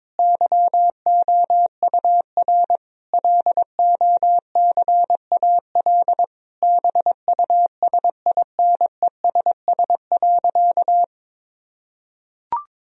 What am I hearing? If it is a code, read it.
Morse code, heard as a keyed tone, about 22 wpm, 700 Hz. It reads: YOUR LOCAL BUSINESS.
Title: Sentences from Top 1000 Words - Review Part 3 - 22wpm